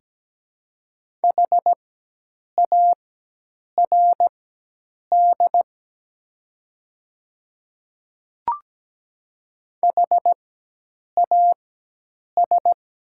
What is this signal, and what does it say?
Morse code, heard as a keyed tone, about 17 wpm, spaced as for 8 wpm, 700 Hz. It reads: HARD HAS
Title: 4x Character Spacing - Top 300 Words - 17wpm